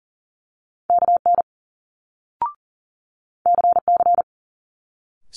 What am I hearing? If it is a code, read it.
Morse code, heard as a keyed tone, about 40 wpm, 700 Hz. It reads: KN CC
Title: Multi-letter Phonemes Quick Fire 40wpm